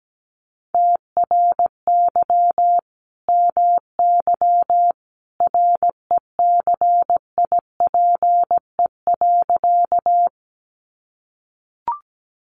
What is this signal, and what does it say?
Morse code, heard as a keyed tone, about 17 wpm, 700 Hz. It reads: TRY MY RECIPE.
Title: Sentences from Top 1200 Words 17wpm